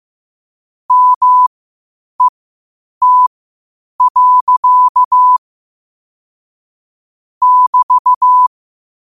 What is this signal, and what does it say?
Morse code, heard as a keyed tone, about 15 wpm, spaced as for 8 wpm, 1000 Hz. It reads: MET. <BT>
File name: qotdfiles / 8wpm